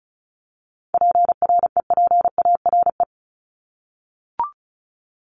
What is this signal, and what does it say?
Morse code, heard as a keyed tone, about 34 wpm, 700 Hz. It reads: PREPARE
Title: ED Suffix - Top 100 Words with Base Words 35wpm